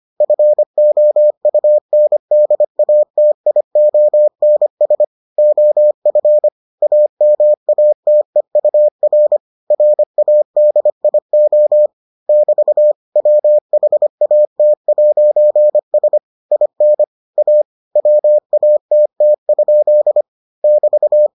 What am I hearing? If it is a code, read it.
Morse code, heard as a keyed tone, about 25 wpm, 600 Hz. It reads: FOUNDATIONS OF AMATEUR RADIO <BT> WHAT'S IN A WATT? <BT>